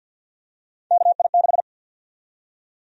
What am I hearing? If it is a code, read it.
Morse code, heard as a keyed tone, about 49 wpm, 700 Hz. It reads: KI6
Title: USA Call Sign Prefixes - 50wpm